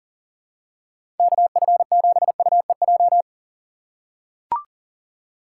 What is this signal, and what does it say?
Morse code, heard as a keyed tone, about 37 wpm, 700 Hz. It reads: KF7UEJ
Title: USA Call Signs - 6 Characters - 4x Character Spacing - 40wpm